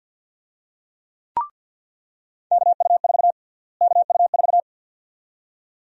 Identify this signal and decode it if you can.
Morse code, heard as a keyed tone, about 46 wpm, 700 Hz. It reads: KU4 KU4
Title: USA Call Sign Prefixes - Repeated 2x - 50wpm